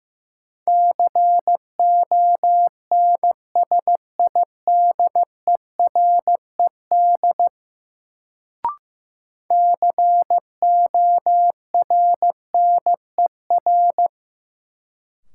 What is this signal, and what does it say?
Morse code, heard as a keyed tone, about 15 wpm, 700 Hz. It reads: CONSIDERED CORNER
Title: Sets of 5 Words - New Word Drill - 15wpm